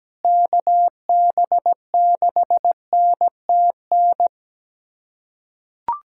Morse code, heard as a keyed tone, about 17 wpm, 700 Hz. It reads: KB6NTN
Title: USA Callsign DE Callsign 17wpm